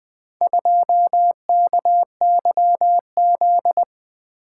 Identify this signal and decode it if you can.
Morse code, heard as a keyed tone, about 24 wpm, 700 Hz. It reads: 2KYZ
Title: Speed-Racing - Four Letter-Number Combinations - 20wpm